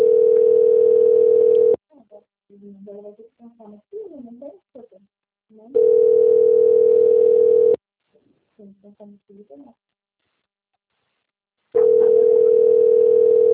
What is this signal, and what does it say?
Ringback tone (North America), 440 + 480 Hz, with speech.
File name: out- - - - - .24765